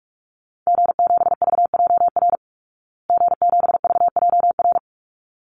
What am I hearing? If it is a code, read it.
Morse code, heard as a keyed tone, about 44 wpm, 700 Hz. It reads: G7VJR G7VJR
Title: Call Signs Repeated 3x - 45wpm